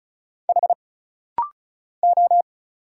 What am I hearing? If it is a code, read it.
Morse code, heard as a keyed tone, about 36 wpm, 700 Hz. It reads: H O
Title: Single Letter-Number Alt - Rapid-Fire 35wpm